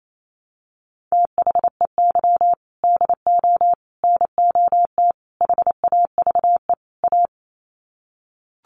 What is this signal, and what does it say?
Morse code, heard as a keyed tone, about 28 wpm, 700 Hz. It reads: THEY DO NOT HAVE A
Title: Sets of 5 Words 28wpm